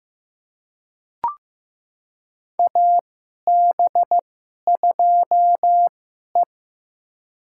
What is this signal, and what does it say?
Morse code, heard as a keyed tone, about 15 wpm, spaced as for 11 wpm, 700 Hz. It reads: AB2E
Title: USA Call Signs - 2x Character Spacing - 15wpm